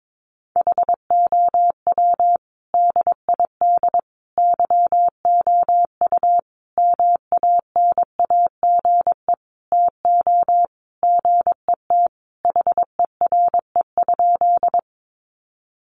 Morse code, heard as a keyed tone, about 22 wpm, 700 Hz. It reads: HOW DID YOU MANAGE TO GET HERE?
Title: Sentences from Top 1400 Words 22wpm